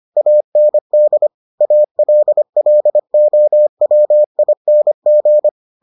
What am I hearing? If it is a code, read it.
Morse code, heard as a keyed tone, about 25 wpm, 600 Hz. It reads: AND ALLOWING